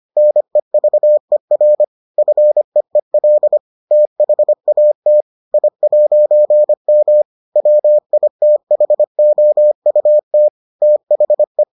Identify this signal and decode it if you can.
Morse code, heard as a keyed tone, about 25 wpm, 600 Hz. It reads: NEVER FEEL THAT I'M WITHOUT THE